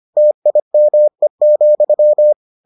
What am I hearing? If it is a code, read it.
Morse code, heard as a keyed tone, about 25 wpm, 600 Hz. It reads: TIME,